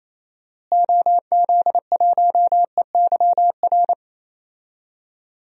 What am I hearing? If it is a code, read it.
Morse code, heard as a keyed tone, about 28 wpm, 700 Hz. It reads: OZ1EYR
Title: DXCC-100 Call Signs - 6 Characters - Repeated 2x - 28wpm